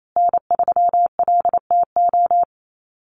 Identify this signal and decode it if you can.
Morse code, heard as a keyed tone, about 28 wpm, 700 Hz. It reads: N3LTO
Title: USA Call Signs Repeated 3x 28wpm